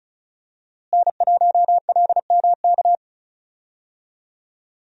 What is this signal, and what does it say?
Morse code, heard as a keyed tone, about 36 wpm, 700 Hz. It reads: N1LMK
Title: USA Call Signs - 5 Characters - Repeated 2x - 35wpm